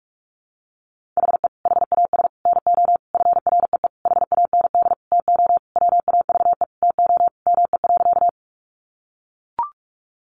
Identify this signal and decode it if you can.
Morse code, heard as a keyed tone, about 44 wpm, 700 Hz. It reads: HE HAS NO FREE HAND TO WAVE TO ME.
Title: Sentences from Top 1400 Words 45wpm